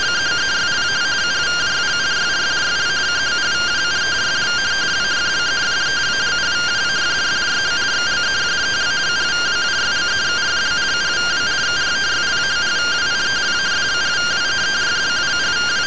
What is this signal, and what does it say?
Two-tone teleprinter signal, about 170 Hz shift.